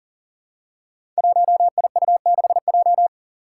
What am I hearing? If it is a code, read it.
Morse code, heard as a keyed tone, about 39 wpm, 700 Hz. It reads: 1IU6J
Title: Five Challenging Letter-Number Combinations 40wpm